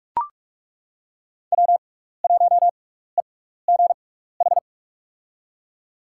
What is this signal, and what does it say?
Morse code, heard as a keyed tone, about 42 wpm, spaced as for 15 wpm, 700 Hz. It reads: W1EGH
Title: USA Call Signs - 6x Character Spacing - 45wpm